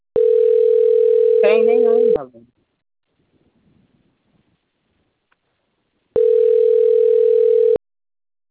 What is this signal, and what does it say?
Ringback tone (North America), 440 + 480 Hz, with speech.
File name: out- - - - - .8260